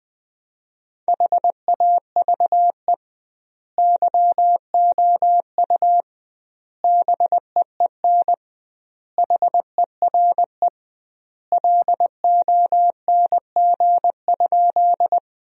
Morse code, heard as a keyed tone, about 20 wpm, 700 Hz. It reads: HAVE YOU BEEN HERE LONG?
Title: 2x Word-Spacing - Sentences from Top 200 Words - 20wpm